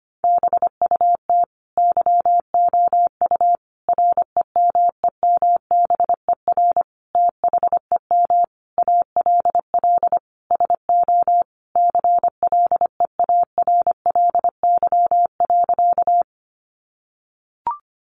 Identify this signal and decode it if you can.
Morse code, heard as a keyed tone, about 25 wpm, 700 Hz. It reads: BUT YOU REMEMBER THEM ALL SO CLEARLY.
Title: Sentences from Top 1300 Words 25wpm